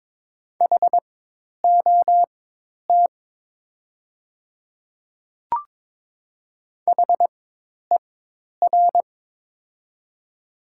Morse code, heard as a keyed tone, about 22 wpm, spaced as for 10 wpm, 700 Hz. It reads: HOT HER
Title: 4x Character Spacing - Top 100 Words - 22wpm